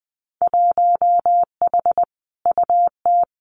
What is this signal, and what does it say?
Morse code, heard as a keyed tone, about 20 wpm, 700 Hz. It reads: 1H UT
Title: Winter Field Day Exchanges 20wpm